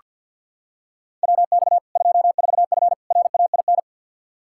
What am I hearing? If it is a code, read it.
Morse code, heard as a keyed tone, about 46 wpm, 700 Hz. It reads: WX 24F RAIN
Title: Practice QSO - Exchange 2 - Info - 50wpm